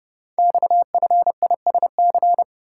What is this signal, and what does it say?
Morse code, heard as a keyed tone, about 28 wpm, 690 Hz. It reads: XFISC
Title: Code Groups - 5 Letters - 30wpm